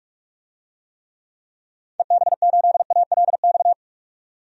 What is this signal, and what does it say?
Morse code, heard as a keyed tone, about 44 wpm, 700 Hz. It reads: EB8ALX
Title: DXCC-100 Call Signs - 6 Characters - 45wpm